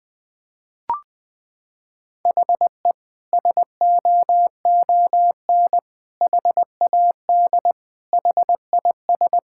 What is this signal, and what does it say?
Morse code, heard as a keyed tone, about 20 wpm, 700 Hz. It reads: HE SOON HAD HIS
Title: Sentences from Top 1600 Words 20wpm